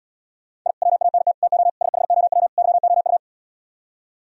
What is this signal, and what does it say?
Morse code, heard as a keyed tone, about 77 wpm, 700 Hz. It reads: I WANT TO SHOW YOU